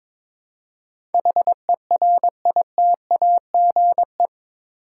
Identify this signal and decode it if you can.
Morse code, heard as a keyed tone, about 22 wpm, 700 Hz. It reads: HERITAGE
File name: practice / 22wpm / Eight Letter Words - 22wpm